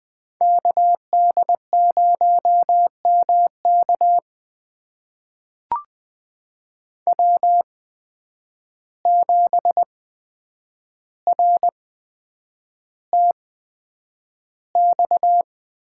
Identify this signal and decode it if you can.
Morse code, heard as a keyed tone, about 20 wpm, spaced as for 9 wpm, 700 Hz. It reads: KD0MK W7RTX
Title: USA Call Signs - 5 Characters - 8x Character Spacing - 20wpm